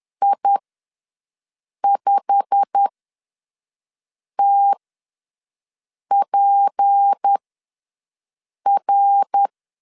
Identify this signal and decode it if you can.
Morse code, heard as a keyed tone, about 10 wpm, spaced as for 5 wpm, 800 Hz. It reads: I5TPR
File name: LEZIONI / 8 YJ45 / 8 YJ45 ws 10wpm 3843